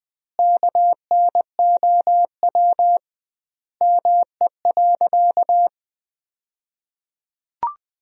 Morse code, heard as a keyed tone, about 20 wpm, 700 Hz. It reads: KNOW ME.
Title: 2x Word-Spacing - Sentences from Top 200 Words - 20wpm